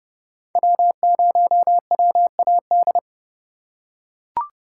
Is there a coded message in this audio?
Yes: Morse code from 0.6 to 3.0 s, heard as a keyed tone, about 28 wpm, 700 Hz. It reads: W0WAD